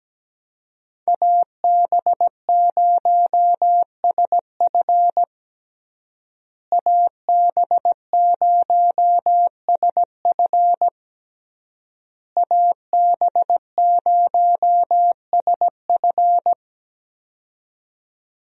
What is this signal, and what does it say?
Morse code, heard as a keyed tone, about 17 wpm, 700 Hz. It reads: AB0SF AB0SF AB0SF
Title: USA Call Signs - 5 Characters - Repeated 3x - 17wpm